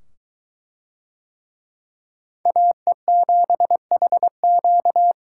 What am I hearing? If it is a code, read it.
Morse code, heard as a keyed tone, about 20 wpm, 700 Hz. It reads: AE7HQ